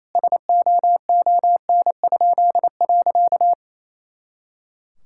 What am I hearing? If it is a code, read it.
Morse code, heard as a keyed tone, about 28 wpm, 700 Hz. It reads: SOON?.
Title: Pangram Challenge v2 - 28wpm